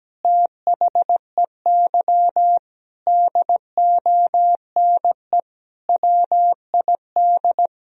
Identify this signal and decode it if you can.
Morse code, heard as a keyed tone, about 17 wpm, 700 Hz. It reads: THEY DONE WID
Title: Sentences from Top 500 Words and Abbreviations 17wpm